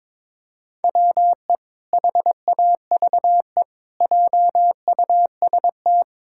Morse code, heard as a keyed tone, about 22 wpm, 700 Hz. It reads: WE HAVE JUST
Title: Pangram Challenge v2 - 22wpm